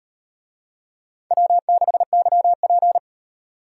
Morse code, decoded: W6YP

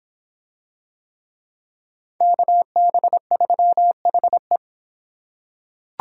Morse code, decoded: KB3HE

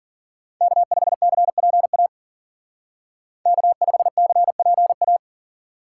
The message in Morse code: K5CPA K5CPA

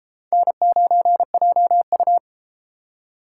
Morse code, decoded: N9JU

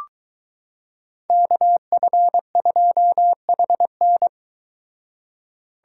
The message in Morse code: KF2HN

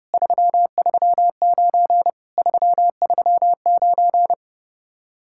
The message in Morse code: 339 339

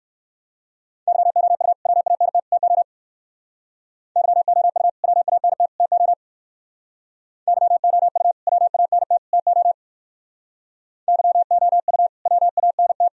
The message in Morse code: YOU WANT TO YOU WANT TO YOU WANT TO YOU WANT

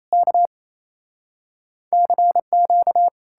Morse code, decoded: K CQ

